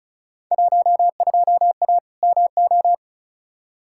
Morse code, heard as a keyed tone, about 34 wpm, 700 Hz. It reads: 12A MO
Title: Field Day Exchanges Encore - 35wpm